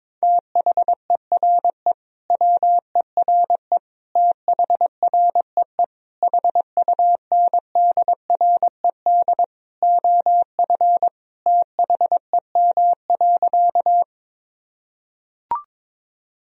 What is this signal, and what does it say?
Morse code, heard as a keyed tone, about 22 wpm, 700 Hz. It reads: THERE WERE THREE HUNDRED OF THEM.
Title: Sentences from Top 1000 Words - Review Part 2 - 22wpm